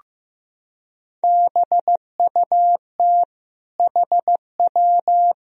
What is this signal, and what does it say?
Morse code, heard as a keyed tone, about 15 wpm, 700 Hz. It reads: BUT HW
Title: Sentences from Top 100 Words and Abbreviations 15wpm